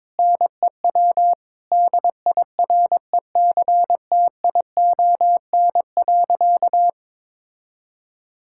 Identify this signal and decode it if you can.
Morse code, heard as a keyed tone, about 22 wpm, 700 Hz. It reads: NEW DIRECTION.
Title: Sentences from Top 1000 Words - Review Part 5 - 22wpm